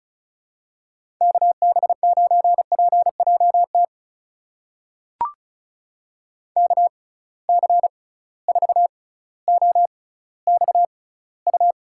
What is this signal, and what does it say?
Morse code, heard as a keyed tone, about 36 wpm, spaced as for 18 wpm, 700 Hz. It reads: KB9PJT KC4OXU
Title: USA Call Signs - 6 Characters - 6x Character Spacing - 35wpm